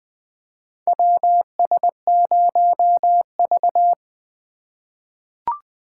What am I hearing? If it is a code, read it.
Morse code, heard as a keyed tone, about 20 wpm, 700 Hz. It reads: WS0V
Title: USA Call Signs - 4 Characters - 2x Character Spacing - 20wpm